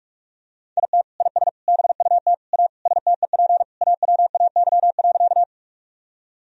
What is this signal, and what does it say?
Morse code, heard as a keyed tone, about 46 wpm, 700 Hz. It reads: IT IS BUT A STEP AWAY.
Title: Sentences from Top 1000 Words - Review Part 2 - 45wpm